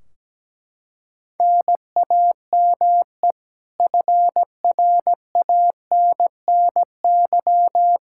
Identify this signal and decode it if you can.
Morse code, heard as a keyed tone, about 17 wpm, 700 Hz. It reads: NAME FRANNY
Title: Practice QSO - Exchange 1 - Beginning - 17wpm